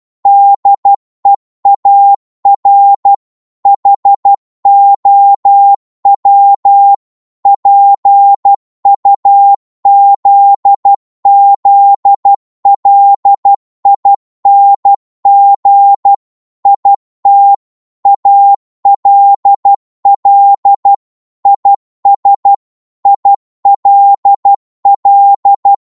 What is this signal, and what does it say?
Morse code, heard as a keyed tone, about 12 wpm, 800 Hz. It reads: DEAR HOW PUZZLING IT ALL IS ILL
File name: 12wpm